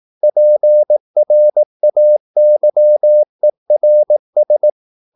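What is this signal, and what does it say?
Morse code, heard as a keyed tone, about 18 wpm, 600 Hz. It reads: PRAYERS
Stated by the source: Morse practice at 18 wpm, English text, from CW med Gnister.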